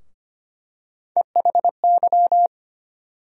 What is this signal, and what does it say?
Morse code, heard as a keyed tone, about 25 wpm, 700 Hz. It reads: EHY
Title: USA Call Sign Postfixes - 3 Letters 25wpm